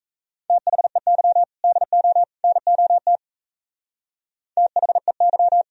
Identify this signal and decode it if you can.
Morse code, heard as a keyed tone, about 35 wpm, 700 Hz. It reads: THEY DO NOT THEY
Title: Speed-Racing - Sets of 3 Words 28wpm